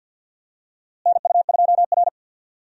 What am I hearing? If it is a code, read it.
Morse code, heard as a keyed tone, about 51 wpm, 700 Hz. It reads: NU2R
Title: USA Call Signs - 4 Characters - Repeated 3x - 50wpm